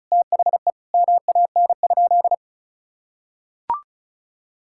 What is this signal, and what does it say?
Morse code, heard as a keyed tone, about 36 wpm, 700 Hz. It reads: THE MAN?
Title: Sentences from Top 1000 Words - Review Part 5 - 35wpm